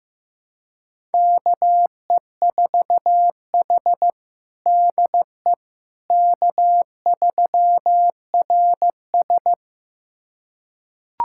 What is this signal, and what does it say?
Morse code, heard as a keyed tone, about 15 wpm, 700 Hz. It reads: KE4H DE K3RS
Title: USA Callsign DE Callsign - 4 Characters - 15wpm